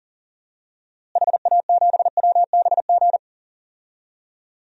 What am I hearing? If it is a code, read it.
Morse code, heard as a keyed tone, about 40 wpm, 700 Hz. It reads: HA7WBG
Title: DXCC-100 Call Signs - 6 Characters - Repeated 2x - 40wpm